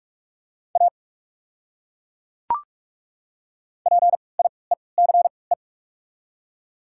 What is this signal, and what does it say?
Morse code, heard as a keyed tone, about 42 wpm, spaced as for 10 wpm, 690 Hz. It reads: APIECE
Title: 3x Character Spacing - Top 500 Words - 45wpm